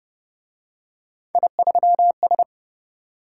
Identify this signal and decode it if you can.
Morse code, heard as a keyed tone, about 29 wpm, 700 Hz. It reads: I3S